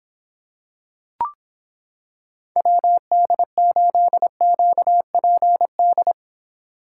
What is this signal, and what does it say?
Morse code, heard as a keyed tone, about 26 wpm, 700 Hz. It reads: WD8QPD